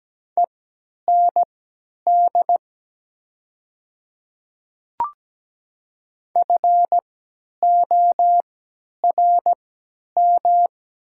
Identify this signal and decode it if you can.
Morse code, heard as a keyed tone, about 17 wpm, spaced as for 10 wpm, 700 Hz. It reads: END FORM
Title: 3x Character Spacing - Top 500 Words - 17wpm